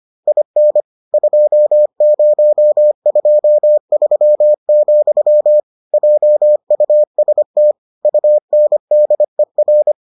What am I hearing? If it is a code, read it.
Morse code, heard as a keyed tone, about 25 wpm, 600 Hz. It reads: IN 2023, JUST UNDER